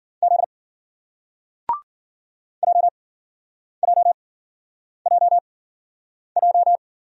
Morse code, heard as a keyed tone, about 51 wpm, spaced as for 9 wpm, 700 Hz. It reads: Z JJJJ